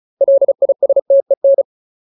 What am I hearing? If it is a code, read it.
Morse code, heard as a keyed tone, about 34 wpm, 550 Hz. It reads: LISTEN
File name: Ultimate Word List / 35WPM Ultimate